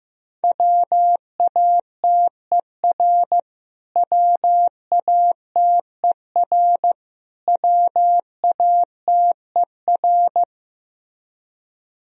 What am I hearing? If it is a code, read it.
Morse code, heard as a keyed tone, about 15 wpm, 700 Hz. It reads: WATER WATER WATER
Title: Top 100 Words Repeated 15wpm